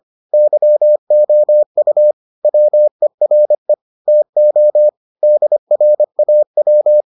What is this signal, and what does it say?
Morse code, heard as a keyed tone, about 25 wpm, 600 Hz. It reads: YOU WERE TO DRAW